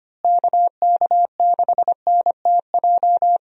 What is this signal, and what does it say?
Morse code, heard as a keyed tone, about 25 wpm, 700 Hz. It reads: KK6NTJ
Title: USA Call Signs - 6 Characters - Repeated 3x - 25wpm